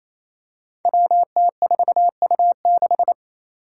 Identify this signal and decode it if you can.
Morse code, heard as a keyed tone, about 28 wpm, 700 Hz. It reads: WT4U6
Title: Five Challenging Letter-Number Combinations 28wpm